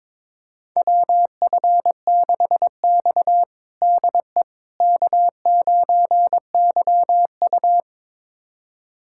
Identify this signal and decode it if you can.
Morse code, heard as a keyed tone, about 22 wpm, 700 Hz. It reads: WF6X DE K9YU
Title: USA Callsign DE Callsign - 4 Characters - 22wpm